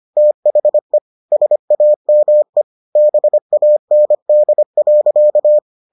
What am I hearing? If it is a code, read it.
Morse code, heard as a keyed tone, about 25 wpm, 600 Hz. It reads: THE SAME BAND.